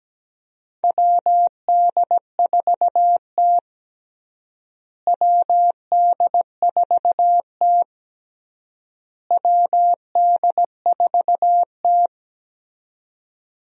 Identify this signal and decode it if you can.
Morse code, heard as a keyed tone, about 17 wpm, 700 Hz. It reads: WD4T WD4T WD4T
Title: USA Call Signs - 4 Characters - Repeated 3x - 17wpm